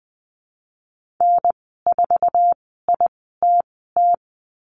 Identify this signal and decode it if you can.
Morse code, heard as a keyed tone, about 20 wpm, spaced as for 15 wpm, 700 Hz. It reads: N4ITT
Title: 062 Lesson 11g - Letter h - Call Signs 20wpm